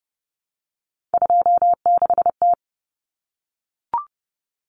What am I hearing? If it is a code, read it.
Morse code, heard as a keyed tone, about 30 wpm, 700 Hz. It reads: 26T
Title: Common Cut Numbers - 3-Digit 30wpm